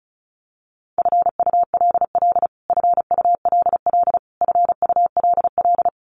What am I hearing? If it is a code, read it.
Morse code, heard as a keyed tone, about 34 wpm, 700 Hz. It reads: FULL FULL FULL
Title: Top 500 Words Repeated 35wpm